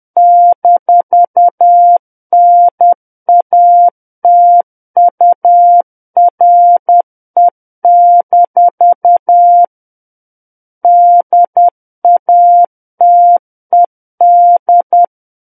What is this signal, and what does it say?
Morse code, heard as a keyed tone, about 10 wpm, 700 Hz. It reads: -NATURE- DATED
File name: Book One / 01-Chapter 1 - 10 WPM